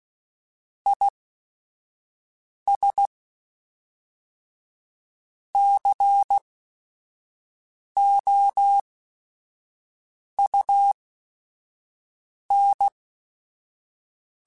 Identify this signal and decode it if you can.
Morse code, heard as a keyed tone, about 16 wpm, spaced as for 5 wpm, 800 Hz. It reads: IS COUN
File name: Technical Subjects / Technical Subjects-mp3s / XC @5